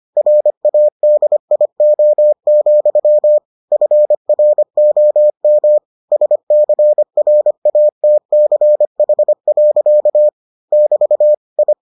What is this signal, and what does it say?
Morse code, heard as a keyed tone, about 25 wpm, 600 Hz. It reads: RADIO, FROM SCRATCH. <BT> I